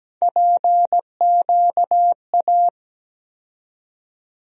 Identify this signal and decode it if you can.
Morse code, heard as a keyed tone, about 17 wpm, 700 Hz. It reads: PQA
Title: USA Call Sign Postfixes - 3 Letters 17wpm